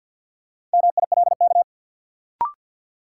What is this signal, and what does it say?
Morse code, heard as a keyed tone, about 53 wpm, 700 Hz. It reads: MILK
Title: Sentences from Top 1500 Words - New Word Drill Alt - 50wpm